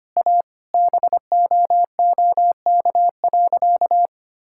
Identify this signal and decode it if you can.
Morse code, heard as a keyed tone, about 25 wpm, 700 Hz. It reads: A BOOK.
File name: practice / 25wpm / Sentences with 4-Word Phrases - 25wpm